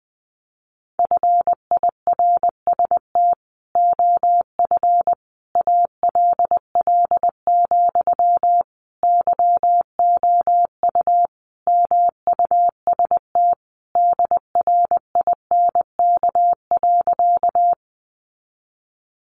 Morse code, heard as a keyed tone, about 20 wpm, 700 Hz. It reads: FIRST OF ALL, YOU MUST DRINK.
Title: Sentences from Top 1500 Words 20wpm